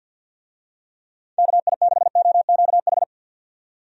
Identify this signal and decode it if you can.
Morse code, heard as a keyed tone, about 53 wpm, 700 Hz. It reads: KI6OQH
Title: USA Call Signs - 6 Characters - 50wpm